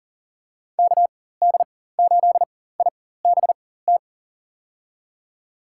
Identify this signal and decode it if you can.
Morse code, heard as a keyed tone, about 40 wpm, spaced as for 19 wpm, 700 Hz. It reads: KD8IBT
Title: USA Call Signs - 4x Character Spacing - 40wpm